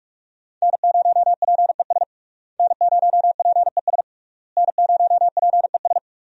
Morse code, heard as a keyed tone, about 46 wpm, 700 Hz. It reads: N0PES N0PES N0PES
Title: USA Call Signs - 5 Characters - Repeated 3x - 45wpm